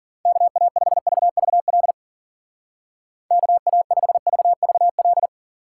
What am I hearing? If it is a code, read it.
Morse code, heard as a keyed tone, about 42 wpm, 700 Hz. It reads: KA5VVL KA5VVL
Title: Speed Racing - USA Call Signs 40wpm